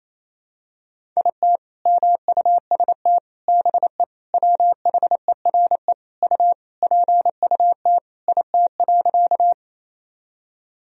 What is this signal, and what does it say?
Morse code, heard as a keyed tone, about 28 wpm, 700 Hz. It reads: IT MUST BE WHERE U PUT IT.
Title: Sentences from Top 200 Words and Abbreviations 28wpm